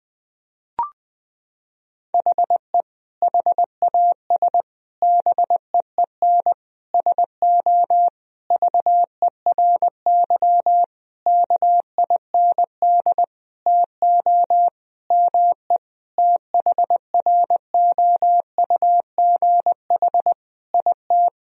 Morse code, heard as a keyed tone, about 20 wpm, 700 Hz. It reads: HE HAS BEEN SO VERY KIND TO ME THROUGH IT
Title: Sentences from Top 1000 Words - Review Part 1 - 20wpm